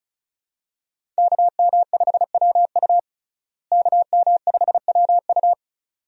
Morse code, heard as a keyed tone, about 34 wpm, 700 Hz. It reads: KM5WU KM5WU
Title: USA Call Signs - 5 Characters - Repeated 2x - 35wpm